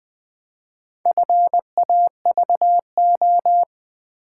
Morse code, heard as a keyed tone, about 20 wpm, 700 Hz. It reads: FAVO
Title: Build A Word - 2 Letters At A Time - 6-Letter Words - 20wpm